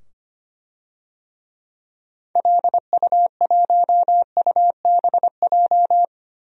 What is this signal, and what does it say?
Morse code, heard as a keyed tone, about 25 wpm, 700 Hz. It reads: LU1UBJ
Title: DXCC-100 Call Signs - 6 Characters - 25wpm